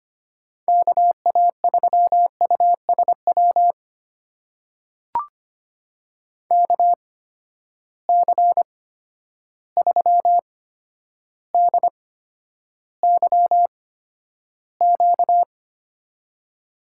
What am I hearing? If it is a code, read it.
Morse code, heard as a keyed tone, about 25 wpm, spaced as for 11 wpm, 700 Hz. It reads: KA3USW KC3DYQ
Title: USA Call Signs - 6 Characters - 8x Character Spacing - 25wpm